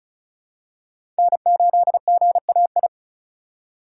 Morse code, heard as a keyed tone, about 36 wpm, 700 Hz. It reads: N8GAI
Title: USA Call Signs - 5 Characters - 6x Character Spacing - 35wpm